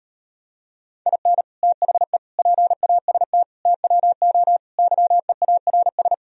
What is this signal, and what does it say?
Morse code, heard as a keyed tone, about 38 wpm, 700 Hz. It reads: IN THE PAST TWO YEARS